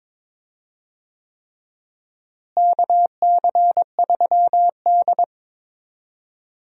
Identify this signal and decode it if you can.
Morse code, heard as a keyed tone, about 22 wpm, 700 Hz. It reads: KC3D
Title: USA Call Signs - 4 Characters - Repeated 3x - 22wpm